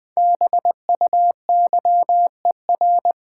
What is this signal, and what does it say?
Morse code, heard as a keyed tone, about 20 wpm, 700 Hz. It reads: BUYER
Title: Five Letter Words - 20wpm